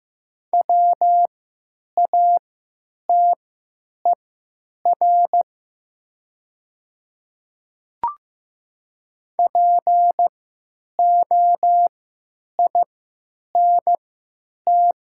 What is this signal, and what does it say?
Morse code, heard as a keyed tone, about 15 wpm, spaced as for 9 wpm, 700 Hz. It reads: WATER POINT
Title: 3x Character Spacing - Top 200 Words - 15wpm